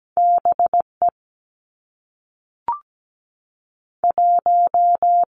Morse code, heard as a keyed tone, about 17 wpm, 700 Hz. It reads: B E 1
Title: Two Challenging Letter-Number Combinations 17wpm